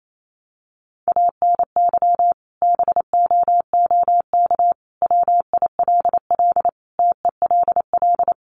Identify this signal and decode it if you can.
Morse code, heard as a keyed tone, about 28 wpm, 700 Hz. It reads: ANY BOOK WILL TELL
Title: Sentences from Top 300 Words 28wpm